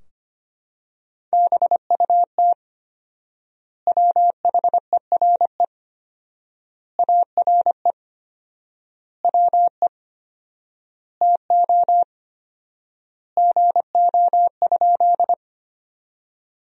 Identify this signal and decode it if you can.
Morse code, heard as a keyed tone, about 25 wpm, 700 Hz. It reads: BUT WHERE ARE WE TO GO?
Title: 4x Word-Spacing - Sentences from Top 200 Words - 25wpm